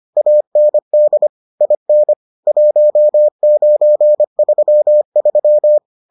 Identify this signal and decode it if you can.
Morse code, heard as a keyed tone, about 25 wpm, 600 Hz. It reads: AND IN 1933